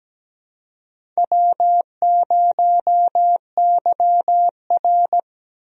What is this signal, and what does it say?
Morse code, heard as a keyed tone, about 17 wpm, 700 Hz. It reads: W0YR
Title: USA Call Signs - 4 Characters - Repeated 2x - 17wpm